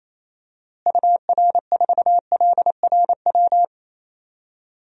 Morse code, heard as a keyed tone, about 28 wpm, 700 Hz. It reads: UR4LRW
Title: DXCC-100 Call Signs - 6 Characters - Repeated 3x - 28wpm